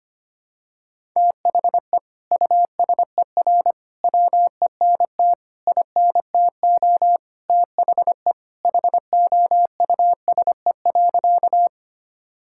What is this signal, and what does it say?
Morse code, heard as a keyed tone, about 25 wpm, 700 Hz. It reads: THE USER WENT INTO THE HOUSE.
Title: Sentences from Top 1000 Words - Review Part 5 - 25wpm